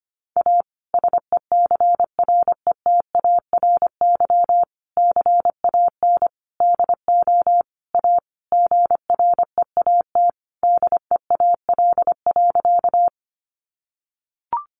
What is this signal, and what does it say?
Morse code, heard as a keyed tone, about 25 wpm, 700 Hz. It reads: A SECRETARY CAN DO A GREAT DEAL.
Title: Sentences from Top 1300 Words 25wpm